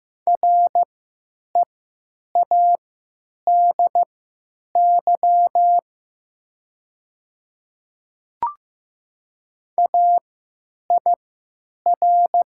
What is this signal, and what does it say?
Morse code, heard as a keyed tone, about 15 wpm, spaced as for 9 wpm, 700 Hz. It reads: READY AIR
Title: 3x Character Spacing - Top 500 Words - 15wpm